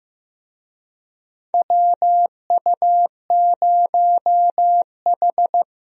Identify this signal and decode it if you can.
Morse code, heard as a keyed tone, about 16 wpm, 700 Hz. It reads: WU0H
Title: USA Call Signs - 4 Characters - Repeated 2x - 15wpm